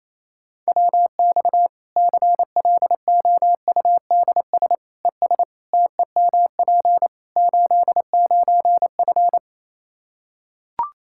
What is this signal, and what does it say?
Morse code, heard as a keyed tone, about 28 wpm, 700 Hz. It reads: WX CLOUDS ES TEMP 89F
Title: Practice QSO - Exchange 2 - Info - 28wpm